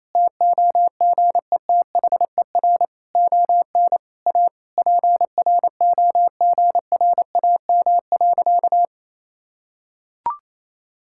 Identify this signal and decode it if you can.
Morse code, heard as a keyed tone, about 27 wpm, 700 Hz. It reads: TOGETHER ON A PROGRAM.
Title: Sentences from Top 1000 Words - Review Part 5 - 28wpm